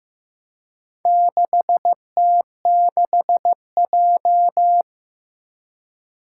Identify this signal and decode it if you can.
Morse code, heard as a keyed tone, about 15 wpm, 700 Hz. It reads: 6T6J
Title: Four Challenging Letter-Number Combinations 15wpm